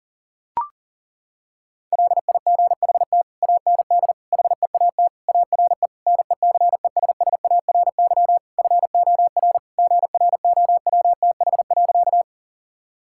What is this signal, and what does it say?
Morse code, heard as a keyed tone, about 40 wpm, 700 Hz. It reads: LIGHT AND HEAT ARE NECESSARY FOR GROWTH.
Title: Sentences from Top 1000 Words - Review Part 5 - 40wpm